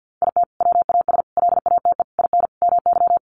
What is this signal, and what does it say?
Morse code, heard as a keyed tone, about 50 wpm, 700 Hz. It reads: IT WAS LATE IN MY
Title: Sentences from Top 1300 Words 50wpm